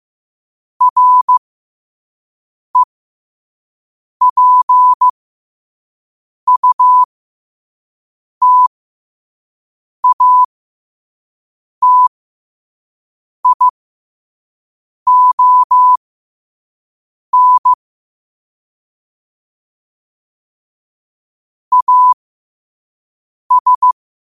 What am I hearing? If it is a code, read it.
Morse code, heard as a keyed tone, about 15 wpm, spaced as for 5 wpm, 1000 Hz. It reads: REPUTATION AS
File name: qotdfiles / 5wpm